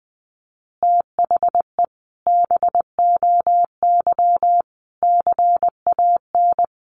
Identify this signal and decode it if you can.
Morse code, heard as a keyed tone, about 20 wpm, 700 Hz. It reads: THE BOY CAN
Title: Additional Easy Phrases 20wpm